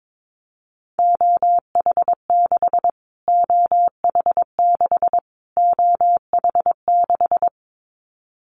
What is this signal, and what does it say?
Morse code, heard as a keyed tone, about 22 wpm, 700 Hz. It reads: OH6 OH6 OH6
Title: Callsign Prefixes for Top 25 DXCC - Repeated 3x - 22wpm